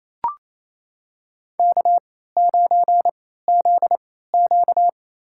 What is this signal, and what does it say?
Morse code, heard as a keyed tone, about 28 wpm, spaced as for 16 wpm, 700 Hz. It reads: K9ZQ
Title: USA Call Signs - 3x Character Spacing - 28wpm